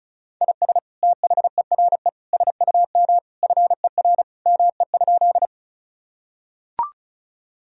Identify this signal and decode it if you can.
Morse code, heard as a keyed tone, about 34 wpm, 700 Hz. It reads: IS THERE SUM FER ME?
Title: Sentences from Top 200 Words and Abbreviations 35wpm